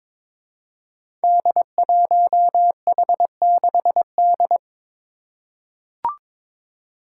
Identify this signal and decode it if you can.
Morse code, heard as a keyed tone, about 23 wpm, 700 Hz. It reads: D1H6D
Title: Five Challenging Letter-Number Combinations 22wpm